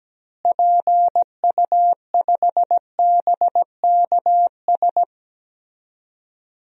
Morse code, heard as a keyed tone, about 17 wpm, 700 Hz. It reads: PU5BKS
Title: DXCC-100 Call Signs - 6 Characters - Repeated 3x - 17wpm